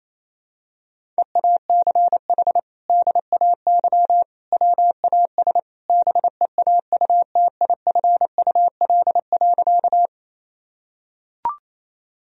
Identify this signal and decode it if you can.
Morse code, heard as a keyed tone, about 27 wpm, 700 Hz. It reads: EACH DAY WAS BEAUTIFUL.
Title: Sentences from Top 1000 Words - Review Part 1 - 28wpm